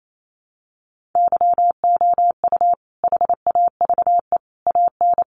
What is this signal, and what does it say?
Morse code, heard as a keyed tone, about 28 wpm, 700 Hz. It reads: YOU HAVE AN
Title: Sentences from Top 1300 Words 28wpm